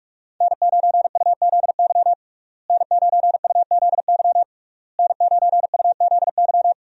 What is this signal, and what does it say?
Morse code, heard as a keyed tone, about 41 wpm, 700 Hz. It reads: N9UZY N9UZY N9UZY
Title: USA Call Signs - 5 Characters - Repeated 3x - 45wpm